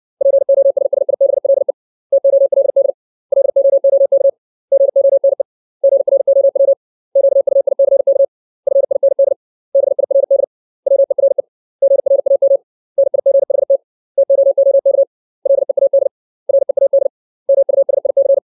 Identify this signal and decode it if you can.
Morse code, heard as a keyed tone, about 60 wpm, 550 Hz. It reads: POSSIBLE TOLD LOOK GONE GROW QUICK FIND BIRD WERE GRAM NIGHT TOOK LEAD READ MUSIC
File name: 632 words / 632 Words 60WPM